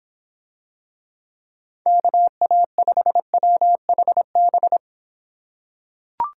Morse code, heard as a keyed tone, about 25 wpm, 700 Hz. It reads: KA5WHB